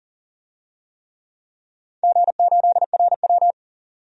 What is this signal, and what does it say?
Morse code, heard as a keyed tone, about 40 wpm, 700 Hz. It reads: G8RW
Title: DXCC-100 Call Signs - 4 Characters - 40wpm